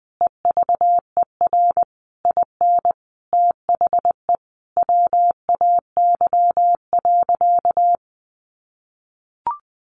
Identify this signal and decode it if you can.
Morse code, heard as a keyed tone, about 20 wpm, 700 Hz. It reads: EVER IN THE WAY.
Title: Sentences from Top 400 Words 20wpm